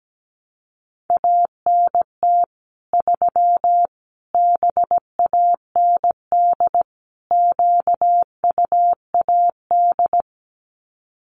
Morse code, heard as a keyed tone, about 17 wpm, 700 Hz. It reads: ANT 3 BAND QUAD
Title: Common QSO Phrases 17wpm